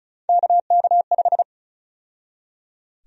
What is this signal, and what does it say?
Morse code, heard as a keyed tone, about 34 wpm, 700 Hz. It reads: KK5
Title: USA Call Sign Prefixes - 35wpm